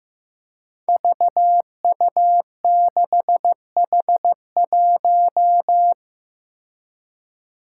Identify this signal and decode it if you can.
Morse code, heard as a keyed tone, about 15 wpm, 700 Hz. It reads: VU6H1
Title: Five Challenging Letter-Number Combinations 15wpm